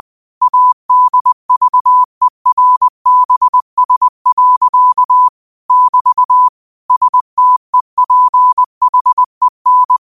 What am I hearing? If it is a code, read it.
Morse code, heard as a keyed tone, about 20 wpm, 1000 Hz. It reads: ADVERBS. <BT> STEPHEN
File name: qotdfiles / 20wpm